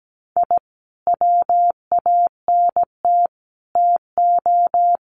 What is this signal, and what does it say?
Morse code, heard as a keyed tone, about 17 wpm, 700 Hz. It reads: I WANT TO
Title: Sets of 5 Words Encore 17wpm